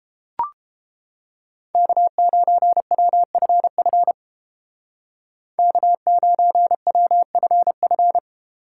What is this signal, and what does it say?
Morse code, heard as a keyed tone, about 29 wpm, 700 Hz. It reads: K9WFF K9WFF